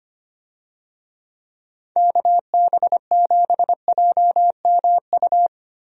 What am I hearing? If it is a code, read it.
Morse code, heard as a keyed tone, about 25 wpm, 700 Hz. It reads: KB7JMU
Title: USA Call Signs - 6 Characters - Repeated 3x - 25wpm